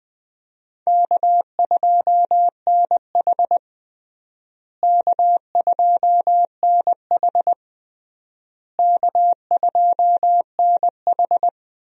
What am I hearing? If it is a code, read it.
Morse code, heard as a keyed tone, about 21 wpm, 700 Hz. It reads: K2NH K2NH K2NH
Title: USA Call Signs - 4 Characters - Repeated 3x - 20wpm